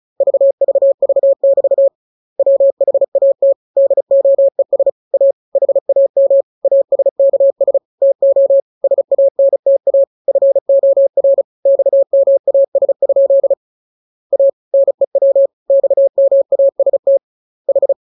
Morse code, heard as a keyed tone, about 34 wpm, 550 Hz. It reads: VVV <BT> WHAT DOES A HAM ASKS TO SANTA FOR XMAS? A NEW XMAST H
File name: jokes / joke 16 35 0000